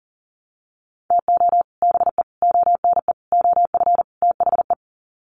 Morse code, heard as a keyed tone, about 40 wpm, 700 Hz. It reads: TO BE ONE OF THE
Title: Sets of 5 Words 40wpm